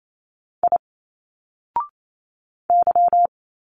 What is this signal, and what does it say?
Morse code, heard as a keyed tone, about 28 wpm, 700 Hz. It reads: I Y
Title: Single Letters 28wpm